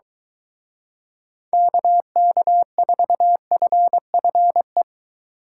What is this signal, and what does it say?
Morse code, heard as a keyed tone, about 23 wpm, 700 Hz. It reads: KK4FFE